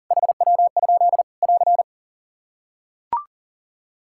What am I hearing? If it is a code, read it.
Morse code, heard as a keyed tone, about 40 wpm, 690 Hz. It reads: HW? <AR>
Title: Practice QSO - Exchange 3 - More Info - 40wpm